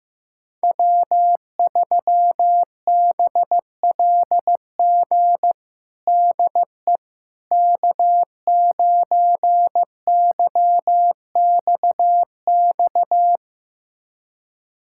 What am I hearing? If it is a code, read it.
Morse code, heard as a keyed tone, about 15 wpm, 700 Hz. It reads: W3BLG DE K9YXX
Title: USA Callsign DE Callsign - 5 Characters - 15wpm